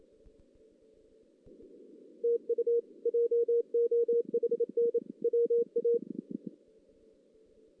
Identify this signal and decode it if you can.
Morse code, heard as a keyed tone, about 31 wpm, 470 Hz. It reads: TU JOHN WA
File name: 2016 JAN CW NAQP / 2016 Jan naqp k5ot - 0-3